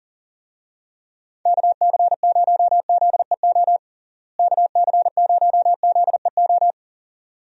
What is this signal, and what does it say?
Morse code, heard as a keyed tone, about 37 wpm, 700 Hz. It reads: KC0ZEO KC0ZEO
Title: USA Call Signs - 6 Characters - Repeated 2x - 40wpm